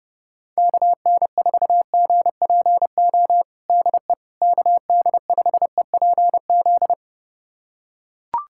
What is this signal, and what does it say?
Morse code, heard as a keyed tone, about 32 wpm, 700 Hz. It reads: KN4GPO DE KD5EPZ
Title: USA Callsign DE Callsign - 6 Characters - 30wpm